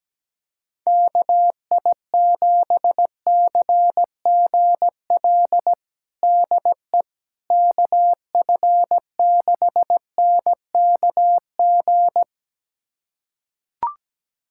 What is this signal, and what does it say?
Morse code, heard as a keyed tone, about 17 wpm, 700 Hz. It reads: KI7CGL DE KF6NKG
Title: USA Callsign DE Callsign - 6 Characters - 17wpm